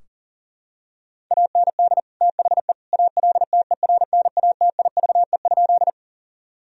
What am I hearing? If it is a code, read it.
Morse code, heard as a keyed tone, about 40 wpm, 700 Hz. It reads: AND THE ALTERNATIVE?
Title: Sentences from Top 1000 Words - Review Part 4 - 40wpm